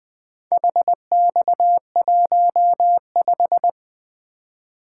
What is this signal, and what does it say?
Morse code, heard as a keyed tone, about 21 wpm, 700 Hz. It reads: HX15
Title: Four Letter-Number Combinations - 20wpm